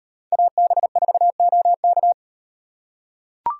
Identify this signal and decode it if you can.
Morse code, heard as a keyed tone, about 37 wpm, 700 Hz. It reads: AB4OK